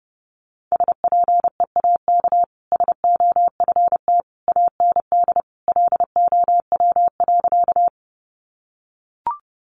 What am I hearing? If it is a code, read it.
Morse code, heard as a keyed tone, about 30 wpm, 700 Hz. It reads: SPEAK SOFT AND LOW.
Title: Sentences from Top 1500 Words 30wpm